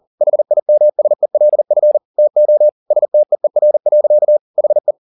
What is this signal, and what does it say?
Morse code, heard as a keyed tone, about 40 wpm, 600 Hz. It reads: HIMSELF TO STEER. HE
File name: the old man and the sea 2 40wpm